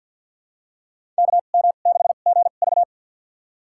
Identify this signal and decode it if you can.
Morse code, heard as a keyed tone, about 52 wpm, spaced as for 36 wpm, 700 Hz. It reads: KM6GV
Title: USA Call Signs - 2x Character Spacing - 50wpm